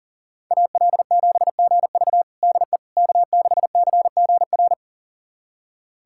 Morse code, heard as a keyed tone, about 40 wpm, 700 Hz. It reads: AL7GV DE K6CGR
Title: USA Callsign DE Callsign - 5 Characters - 40wpm